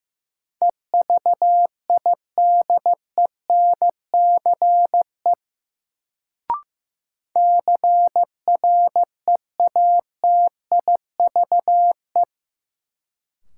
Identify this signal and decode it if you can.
Morse code, heard as a keyed tone, about 15 wpm, 700 Hz. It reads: EVIDENCE CREATIVE
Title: Sentences from Top 1300 Words - New Word Drill Alt - 15wpm